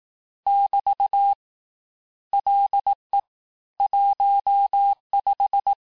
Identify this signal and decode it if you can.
Morse code, heard as a keyed tone, about 18 wpm, 800 Hz. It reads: <BT> LE 15
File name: cours-du-lundi files / cours-1852-texte-clair-vitesse-17MPM-puis-22MPM